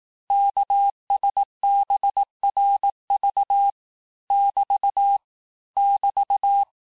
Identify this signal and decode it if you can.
Morse code, heard as a keyed tone, about 18 wpm, 800 Hz. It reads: KSBRV <BT> <BT>
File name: cours-du-lundi files / cours-1594--teste-chiffre-vitesse-17mpm